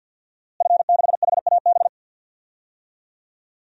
Morse code, heard as a keyed tone, about 51 wpm, 700 Hz. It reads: F6HAB
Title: DXCC-100 Call Signs - 5 Characters - Repeated 3x - 50wpm